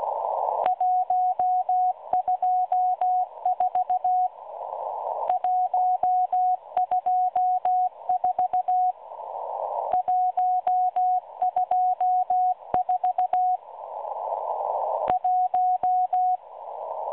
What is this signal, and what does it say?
Morse code, heard as a keyed tone, about 16 wpm, 730 Hz, with radio noise. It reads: 124 124 124 1
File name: M12 - 8116 kHz CW 2-11-14